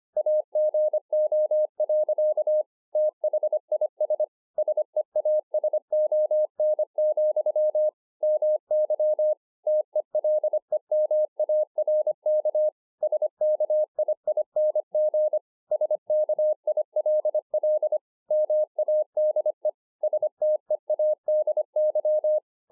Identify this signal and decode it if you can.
Morse code, heard as a keyed tone, about 25 wpm, 610 Hz. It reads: AGO. THIS SEASON, MY TELEMARK SKIING SKILL MADE STEADY